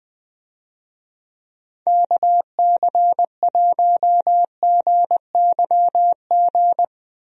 Morse code, heard as a keyed tone, about 20 wpm, 700 Hz. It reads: KC1GYG
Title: USA Call Signs - 6 Characters - Repeated 2x - 20wpm